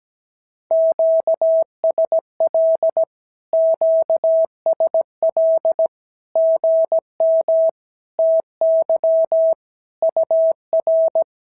Morse code, heard as a keyed tone, about 17 wpm, 650 Hz. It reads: QSL QSL GM TY UR
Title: SOTA QSO Practice 17wpm